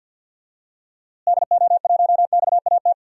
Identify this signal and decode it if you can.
Morse code, heard as a keyed tone, about 48 wpm, 700 Hz. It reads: DO1XAT
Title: DXCC-100 Call Signs - 6 Characters - Repeated 2x - 50wpm